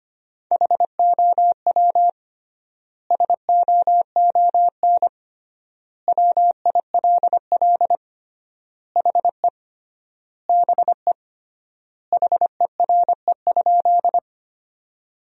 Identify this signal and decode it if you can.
Morse code, heard as a keyed tone, about 25 wpm, 700 Hz. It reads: HOW SOON WILL HE BE HERE?
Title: 3x Word-Spacing - Sentences from Top 400 Words - 25wpm